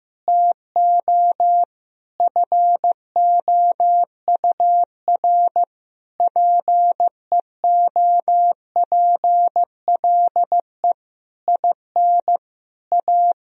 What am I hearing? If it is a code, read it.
Morse code, heard as a keyed tone, about 15 wpm, 700 Hz. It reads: TO FOUR PEOPLE IN A
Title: Sentences from Top 1000 Words - Review Part 3 - 15wpm